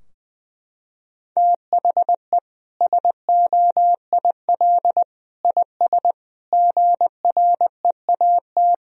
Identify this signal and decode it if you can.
Morse code, heard as a keyed tone, about 21 wpm, 700 Hz. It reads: THE SOIL IS GREAT
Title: Sentences from Top 1000 Words - Review Part 4 - 20wpm